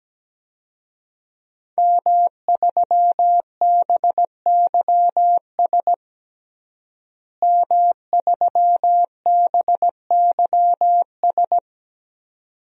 Morse code, heard as a keyed tone, about 17 wpm, 700 Hz. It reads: M3BYS M3BYS
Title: DXCC-100 Call Signs - 5 Characters - Repeated 2x - 17wpm